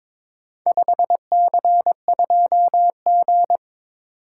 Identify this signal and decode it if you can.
Morse code, heard as a keyed tone, about 22 wpm, 700 Hz. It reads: 5C2G
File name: practice / 22wpm / Four Letter-Number Combinations - 22wpm